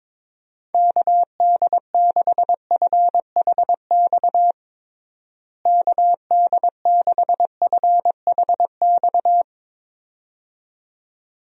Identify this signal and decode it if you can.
Morse code, heard as a keyed tone, about 22 wpm, 700 Hz. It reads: KD6FHX KD6FHX
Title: USA Call Signs - 6 Characters - Repeated 2x - 22wpm